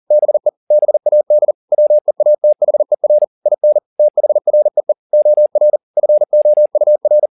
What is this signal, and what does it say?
Morse code, heard as a keyed tone, about 40 wpm, 600 Hz. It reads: BE BAD WEATHER IN THREE OR FOUR
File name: the old man and the sea 2 40wpm